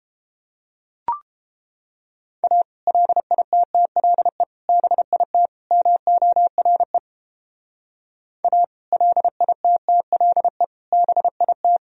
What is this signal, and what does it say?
Morse code, heard as a keyed tone, about 29 wpm, 700 Hz. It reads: A LITTLE BIT MORE A LITTLE BIT